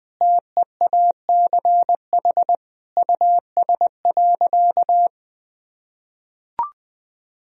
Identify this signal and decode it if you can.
Morse code, heard as a keyed tone, about 20 wpm, 700 Hz. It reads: TEACH US.
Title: Sentences from Top 1000 Words - Review Part 2 - 20wpm